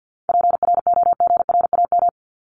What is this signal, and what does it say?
Morse code, heard as a keyed tone, about 48 wpm, 700 Hz. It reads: PROGRAM
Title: ES:S Suffix - Top 100 Words with Base Words 50wpm